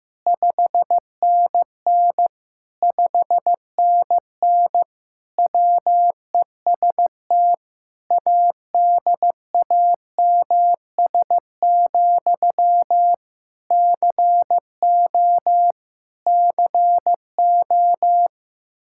Morse code, heard as a keyed tone, about 15 wpm, 700 Hz. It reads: 5NN 5NN WEST ADAMS, CO CO
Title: Practice QSO - Exchange 1 - Beginning - 15wpm